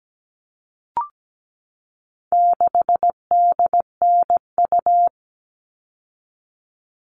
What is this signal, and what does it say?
Morse code, heard as a keyed tone, about 17 wpm, 700 Hz. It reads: 6DNU
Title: All Cut Numbers - 4-Digit 17wpm